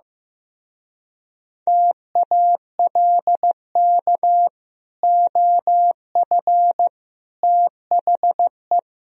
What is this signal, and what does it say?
Morse code, heard as a keyed tone, about 15 wpm, 700 Hz. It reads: TALK OF THE
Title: Speed-Racing - Sets of 4 Words 15wpm